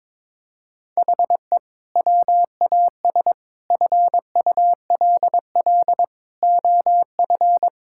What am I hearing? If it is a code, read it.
Morse code, heard as a keyed tone, about 22 wpm, 700 Hz. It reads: HE WAS FULL OF
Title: Sentences from Top 1000 Words - Review Part 6 - 22wpm